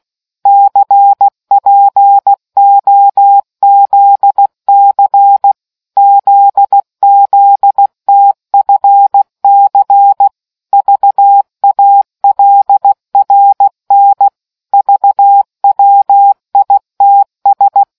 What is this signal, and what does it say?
Morse code, heard as a keyed tone, about 16 wpm, 800 Hz. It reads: CPOZC ZZTFC VALRN VWITS